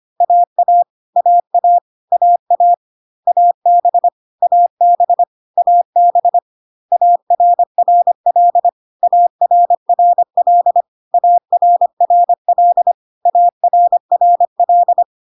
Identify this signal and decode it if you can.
Morse code, heard as a keyed tone, about 25 wpm, 700 Hz. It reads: AA AA AA AB AB AB ARRL ARRL ARRL ARRL
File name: CW abbreviations alphabetical repeated 3 times 25 25wpm